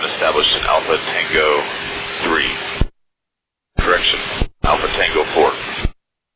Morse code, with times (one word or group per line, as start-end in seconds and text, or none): none